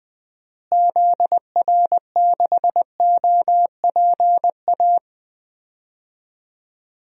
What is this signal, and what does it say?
Morse code, heard as a keyed tone, about 20 wpm, 700 Hz. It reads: ZR6OPA
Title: DXCC-100 Call Signs - 6 Characters - Repeated 2x - 20wpm